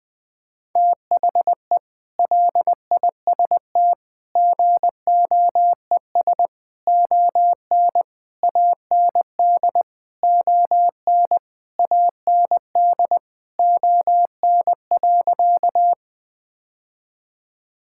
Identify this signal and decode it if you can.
Morse code, heard as a keyed tone, about 20 wpm, 700 Hz. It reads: THE LIST GOES ON AND ON AND ON.
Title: Sentences with 8-Word Phrases - Spoken Beginning - 20wpm